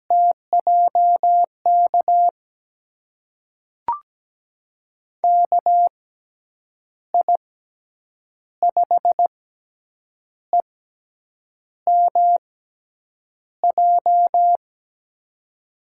Morse code, heard as a keyed tone, about 17 wpm, spaced as for 7 wpm, 700 Hz. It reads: TJK KI5EMJ
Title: USA Call Signs - 6 Characters - 6x Character Spacing - 17wpm